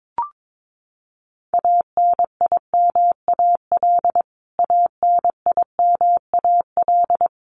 Morse code, heard as a keyed tone, about 22 wpm, 700 Hz. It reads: ANIMAL ANIMAL
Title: Top 500 Words Repeated 22wpm